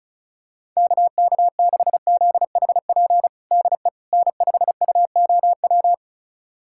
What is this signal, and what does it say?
Morse code, heard as a keyed tone, about 34 wpm, 700 Hz. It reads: KK6ZHP DE N5UOW
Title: USA Callsign DE Callsign 35wpm